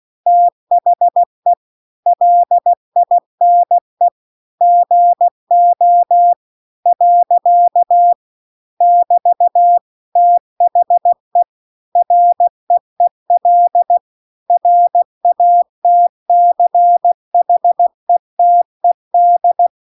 Morse code, heard as a keyed tone, about 16 wpm, 700 Hz. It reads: THE LINE GO . <BT> THE REEL RATCHETED
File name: CW-The-Big-Twoheaded-River-2-16WPM